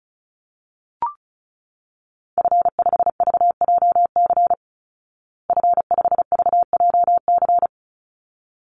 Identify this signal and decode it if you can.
Morse code, heard as a keyed tone, about 34 wpm, 700 Hz. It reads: F5VJC F5VJC
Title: Call Signs Repeated 3x - 35wpm